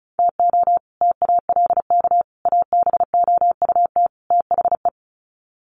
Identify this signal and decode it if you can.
Morse code, heard as a keyed tone, about 34 wpm, 700 Hz. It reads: TO TALK ABOUT THE
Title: Sets of 4 Words 35wpm